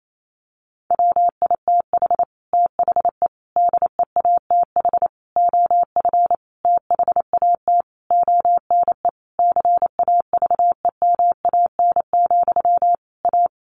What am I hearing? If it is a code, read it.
Morse code, heard as a keyed tone, about 28 wpm, 700 Hz. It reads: WITH THE DEATH OF THAT ONE CAVEMAN, A
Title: A Sound Of Thunder 28wpm